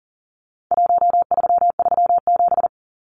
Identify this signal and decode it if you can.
Morse code, heard as a keyed tone, about 39 wpm, 700 Hz. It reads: 1337
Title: Numbers - 4-Digit 40wpm